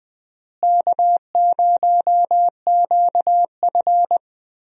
Morse code, heard as a keyed tone, about 20 wpm, 700 Hz. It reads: K0QF
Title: USA Call Signs - 4 Characters - Repeated 2x - 20wpm